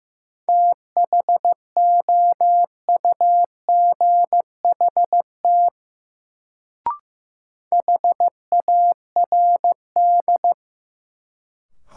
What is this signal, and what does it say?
Morse code, heard as a keyed tone, about 15 wpm, 700 Hz. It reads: THOUGHT HARD
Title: Sentences from Top 300 Words - New Word Drill - 15wpm